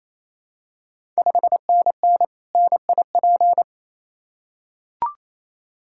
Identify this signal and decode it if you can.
Morse code, heard as a keyed tone, about 28 wpm, 700 Hz. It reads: 5NN NIP
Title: Ontario QSO Party Exchange 28wpm